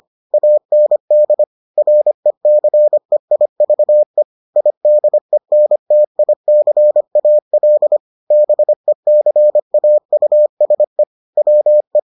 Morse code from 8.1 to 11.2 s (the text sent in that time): BECAUSE